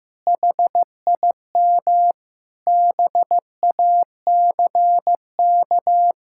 Morse code, heard as a keyed tone, about 15 wpm, 700 Hz. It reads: HIM BACK
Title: Sentences from Top 1000 Words - Review Part 6 - 15wpm